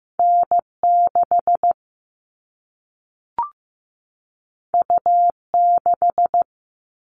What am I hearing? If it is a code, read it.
Morse code, heard as a keyed tone, about 15 wpm, 700 Hz. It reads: N6 U6
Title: All Cut Numbers - 2-Digit 15wpm